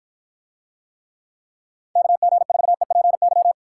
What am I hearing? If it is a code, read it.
Morse code, heard as a keyed tone, about 53 wpm, 700 Hz. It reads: KG4EPY